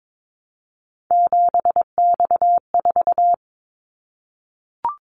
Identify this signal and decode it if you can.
Morse code, heard as a keyed tone, about 22 wpm, 700 Hz. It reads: 7X4
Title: Callsign Prefixes for Top 100 DXCC - 22wpm